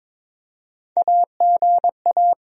Morse code, heard as a keyed tone, about 22 wpm, 700 Hz. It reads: AGA
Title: Sentences from Top 1000 Words - Review Part 3 - 22wpm